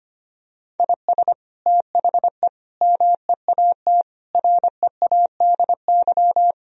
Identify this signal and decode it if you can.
Morse code, heard as a keyed tone, about 25 wpm, 700 Hz. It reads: IS THE MEAT READY
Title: Sentences from Top 1000 Words - Review Part 3 - 25wpm